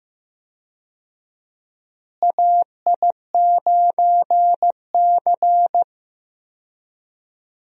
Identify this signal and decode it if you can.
Morse code, heard as a keyed tone, about 15 wpm, 700 Hz. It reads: AI9C
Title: USA Call Signs - 4 Characters - Repeated 3x - 15wpm